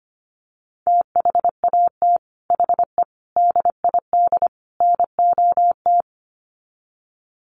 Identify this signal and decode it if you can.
Morse code, heard as a keyed tone, about 25 wpm, 700 Hz. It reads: THAT HE DID NOT
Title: Sets of 4 Words 25wpm